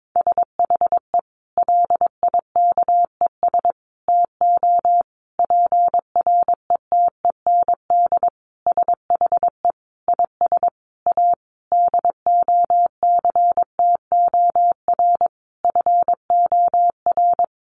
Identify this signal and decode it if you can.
Morse code, heard as a keyed tone, about 22 wpm, 700 Hz. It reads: SHE LIKES TO PRETEND SHE IS A DOCTOR FOR
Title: Additional Easy Phrases 22wpm